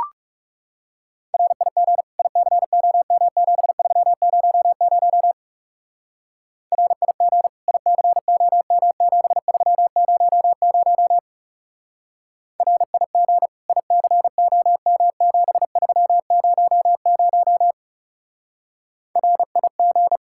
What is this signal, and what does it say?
Morse code, heard as a keyed tone, about 36 wpm, 700 Hz. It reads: RIG ICOM7300 RIG ICOM7300 RIG ICOM7300 RIG
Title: Speed-Racing Common QSO Phrases Encore - 30wpm